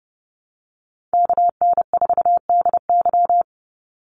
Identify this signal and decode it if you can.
Morse code, heard as a keyed tone, about 30 wpm, 700 Hz. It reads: KN4DY
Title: USA Call Signs 30wpm